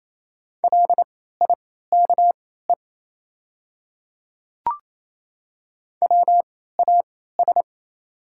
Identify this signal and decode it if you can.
Morse code, heard as a keyed tone, about 28 wpm, spaced as for 16 wpm, 690 Hz. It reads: LIKE WAS
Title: 3x Character Spacing - Top 100 Words - 28wpm